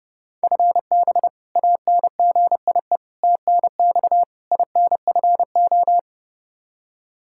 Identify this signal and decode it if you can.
Morse code, heard as a keyed tone, about 29 wpm, 700 Hz. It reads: FB ANGIE TNX INFO
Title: K1USN SST QSO Practice 30wpm